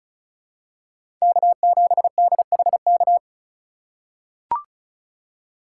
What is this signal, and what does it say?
Morse code, heard as a keyed tone, about 34 wpm, 700 Hz. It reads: K7DHK
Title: USA Call Signs - 5 Characters - 4x Character Spacing - 35wpm